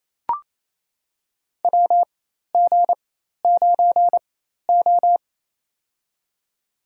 Morse code, heard as a keyed tone, about 28 wpm, spaced as for 13 wpm, 700 Hz. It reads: WG9O
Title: USA Call Signs - 4 Characters - 4x Character Spacing - 28wpm